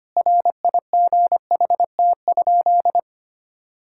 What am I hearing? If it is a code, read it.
Morse code, heard as a keyed tone, about 25 wpm, 700 Hz. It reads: RIGHT?
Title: Sentences from Top 1000 Words - Review Part 5 - 25wpm